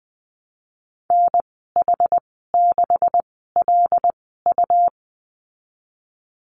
Morse code, heard as a keyed tone, about 20 wpm, spaced as for 15 wpm, 700 Hz. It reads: NH6LU
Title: 167 Lesson 27g - Number 7 - Call Signs 20wpm